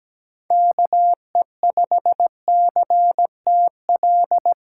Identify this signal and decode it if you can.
Morse code, heard as a keyed tone, about 17 wpm, 700 Hz. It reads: KE5CTL
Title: USA Call Signs - 6 Characters - 4x Character Spacing - 17wpm